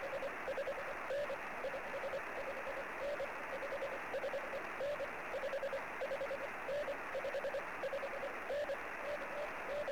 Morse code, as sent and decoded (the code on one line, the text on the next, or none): none